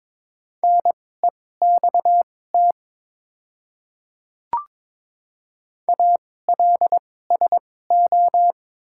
Morse code, heard as a keyed tone, about 22 wpm, spaced as for 16 wpm, 700 Hz. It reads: NEXT ALSO
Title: 2x Character Spacing - Top 300 Words - 22wpm